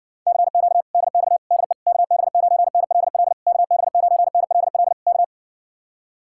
Morse code, heard as a keyed tone, about 69 wpm, 700 Hz. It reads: CQ DX DE KB9NLL KB9NLL K